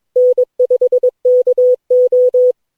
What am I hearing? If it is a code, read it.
Morse code, heard as a keyed tone, about 22 wpm, 500 Hz. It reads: N5KO
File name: cw / Session9 cw training